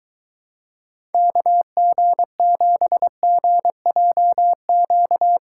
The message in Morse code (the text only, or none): KG7GJQ